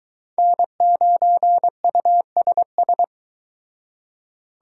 Morse code, heard as a keyed tone, about 22 wpm, 700 Hz. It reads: N9USS